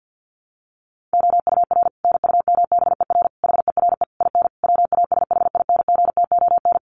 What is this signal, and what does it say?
Morse code, heard as a keyed tone, about 48 wpm, 700 Hz. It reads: OUR NUMBER HERE IN WASHINGTON
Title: Sets of 5 Words Encore2 50wpm